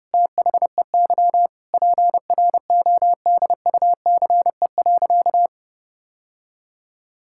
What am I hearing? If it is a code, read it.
Morse code, heard as a keyed tone, about 30 wpm, 700 Hz. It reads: THEY PRODUCE.
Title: Sentences from Top 1000 Words - Review Part 5 - 30wpm